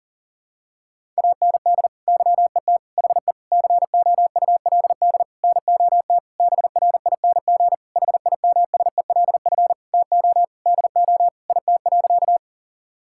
Morse code, heard as a keyed tone, about 40 wpm, 700 Hz. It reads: AND YET HE COULD NOT BRING HIMSELF TO DO IT.
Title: Sentences from Top 1600 Words 40wpm